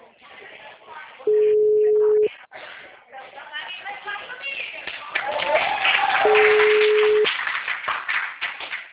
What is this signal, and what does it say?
Ringback tone (Europe, 425 Hz), with speech.